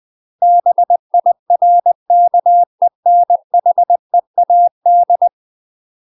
Morse code, heard as a keyed tone, about 20 wpm, 700 Hz. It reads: BIRKENHEAD
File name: UK MASTER QSOs TRAINING / 20WPM / UK MASTER QSOs 0005